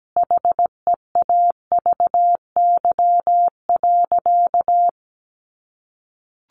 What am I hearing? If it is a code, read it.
Morse code, heard as a keyed tone, about 17 wpm, 700 Hz. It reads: HEAVY.
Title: Sentences from Top 1400 Words 17wpm